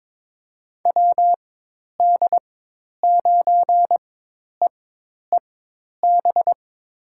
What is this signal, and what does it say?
Morse code, heard as a keyed tone, about 22 wpm, spaced as for 10 wpm, 700 Hz. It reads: WD9EEB
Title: USA Call Signs - 6 Characters - 4x Character Spacing - 22wpm